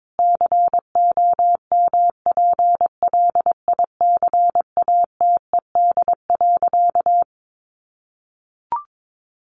Morse code, heard as a keyed tone, about 22 wpm, 700 Hz. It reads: COMPLICATED.
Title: Sentences from Top 1200 Words 22wpm